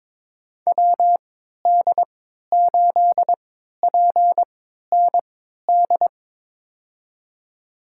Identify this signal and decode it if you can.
Morse code, heard as a keyed tone, about 22 wpm, spaced as for 12 wpm, 700 Hz. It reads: WD8PND
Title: USA Call Signs - 6 Characters - 3x Character Spacing - 22wpm